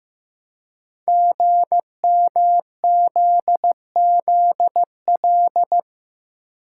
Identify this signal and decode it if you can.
Morse code, heard as a keyed tone, about 15 wpm, 700 Hz. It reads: GMZZL
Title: Code Groups - 5 Letters - 15wpm